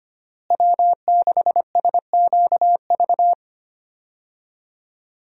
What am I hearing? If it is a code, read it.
Morse code, heard as a keyed tone, about 25 wpm, 700 Hz. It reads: W6SQV
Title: USA Call Signs - 5 Characters - Repeated 3x - 25wpm